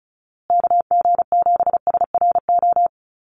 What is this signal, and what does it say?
Morse code, heard as a keyed tone, about 34 wpm, 700 Hz. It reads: KG7SRO
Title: USA Call Signs Repeated 3x 35wpm